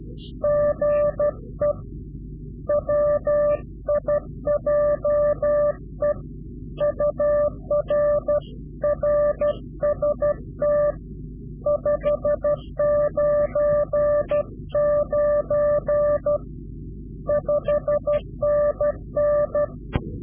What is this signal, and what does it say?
Morse code, heard as a keyed tone, about 12 wpm, 590 Hz. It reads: GE WIJE UR RST 599 5NN
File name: CW-QSO / Directional Call / Directional QSO